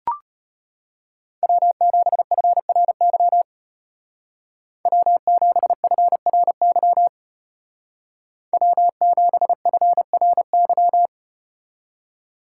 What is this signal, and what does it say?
Morse code, heard as a keyed tone, about 33 wpm, 700 Hz. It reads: W7FRY W7FRY W7FRY